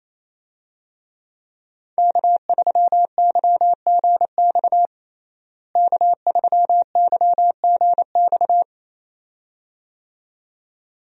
Morse code, heard as a keyed tone, about 28 wpm, 700 Hz. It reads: K3YGX K3YGX
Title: USA Call Signs - 5 Characters - Repeated 2x - 28wpm